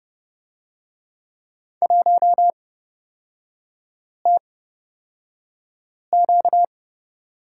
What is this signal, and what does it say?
Morse code, heard as a keyed tone, about 29 wpm, spaced as for 5 wpm, 700 Hz. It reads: 1TQ
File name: Lesson 38e - Letter q - Three Characters 30wpm